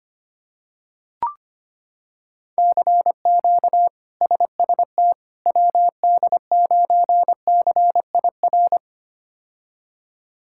Morse code, heard as a keyed tone, about 25 wpm, 700 Hz. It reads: CQ SST WD9CIR
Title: K1USN SST QSO Practice - 25wpm